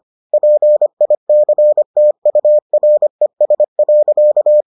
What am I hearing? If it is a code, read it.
Morse code, heard as a keyed tone, about 25 wpm, 600 Hz. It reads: PICTURES.